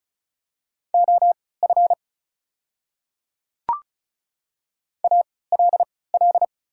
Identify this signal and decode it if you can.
Morse code, heard as a keyed tone, about 36 wpm, spaced as for 20 wpm, 700 Hz. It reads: OF ALL
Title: 3x Character Spacing - Top 200 Words - 35wpm